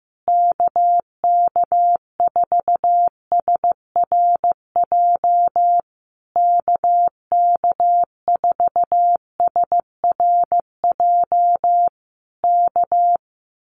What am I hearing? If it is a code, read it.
Morse code, heard as a keyed tone, about 15 wpm, 700 Hz. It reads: KK4SRJ KK4SRJ K
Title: Common QSO Phrases Encore 15wpm